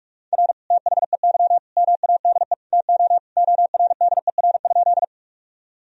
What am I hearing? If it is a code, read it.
Morse code, heard as a keyed tone, about 45 wpm, 700 Hz. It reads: R THEY MADE TO ORDER?
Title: Sentences from Top 400 Words and Abbreviations 45wpm